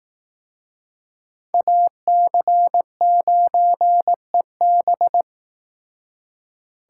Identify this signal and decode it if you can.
Morse code, heard as a keyed tone, about 18 wpm, 700 Hz. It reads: AC9EB
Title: Speed Racing - USA Call Signs 15wpm